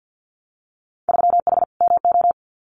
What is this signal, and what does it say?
Morse code, heard as a keyed tone, about 50 wpm, 700 Hz. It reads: 3H MO
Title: Winter Field Day Exchanges Encore 50wpm